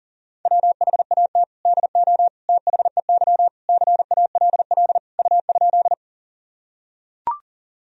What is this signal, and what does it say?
Morse code, heard as a keyed tone, about 40 wpm, 700 Hz. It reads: WHAT DO THEY CALL U?
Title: Sentences from Top 100 Words and Abbreviations 40wpm